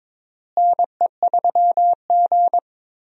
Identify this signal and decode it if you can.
Morse code, heard as a keyed tone, about 21 wpm, 700 Hz. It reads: NE3G
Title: USA Call Signs - 4 Characters - Repeated 3x - 22wpm